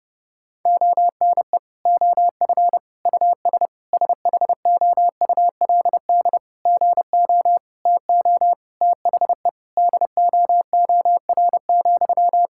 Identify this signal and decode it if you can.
Morse code, heard as a keyed tone, about 30 wpm, 700 Hz. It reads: ONE OF US SHOULD GO TO THE DOOR,
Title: Sentences from Top 1600 Words 30wpm